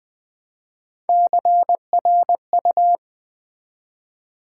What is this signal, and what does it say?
Morse code, heard as a keyed tone, about 20 wpm, 700 Hz. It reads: CRU
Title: Build A Word - 1 Letter At A Time - 5-Letter Words - 20wpm